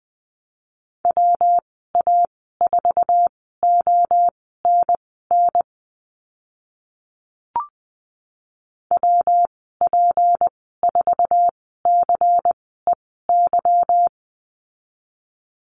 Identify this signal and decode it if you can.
Morse code, heard as a keyed tone, about 20 wpm, spaced as for 14 wpm, 700 Hz. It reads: WA4ONN WP4CEY
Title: 161 Lesson 26g - Letter g - Call Signs 20wpm